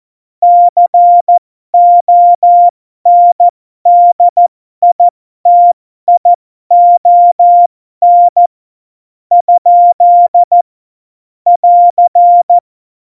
Morse code, heard as a keyed tone, about 14 wpm, spaced as for 11 wpm, 700 Hz. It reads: CONDITION? <AR>
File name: Morsekurs / Witze